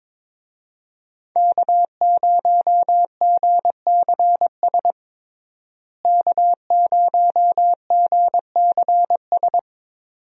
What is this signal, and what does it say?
Morse code, heard as a keyed tone, about 22 wpm, 700 Hz. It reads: K0GCS K0GCS
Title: USA Call Signs - 5 Characters - Repeated 2x - 22wpm